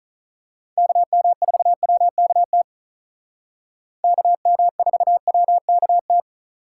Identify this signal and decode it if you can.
Morse code, heard as a keyed tone, about 36 wpm, 700 Hz. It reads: KM4WKT KM4WKT
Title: Speed Racing - USA Call Signs 35wpm